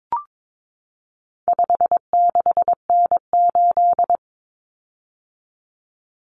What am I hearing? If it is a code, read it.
Morse code, heard as a keyed tone, about 22 wpm, 700 Hz. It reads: 56N8
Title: Common Cut Numbers - 4-Digit 22wpm